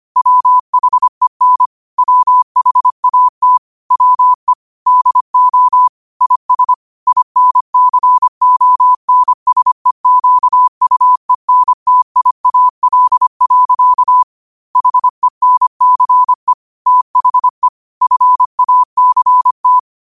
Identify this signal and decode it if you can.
Morse code, heard as a keyed tone, about 25 wpm, 1000 Hz. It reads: WHEN WHAT WE DO IS INCONSEQUENTIAL. HENCE THE FACT
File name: qotdfiles / 25wpm